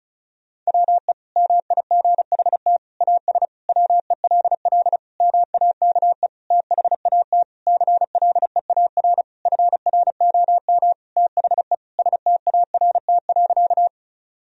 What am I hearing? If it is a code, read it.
Morse code, heard as a keyed tone, about 36 wpm, 700 Hz. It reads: WE MIGHT AS WELL MAKE THAT CLEAR FROM THE START.
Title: Sentences from Top 1000 Words - Review Part 1 - 35wpm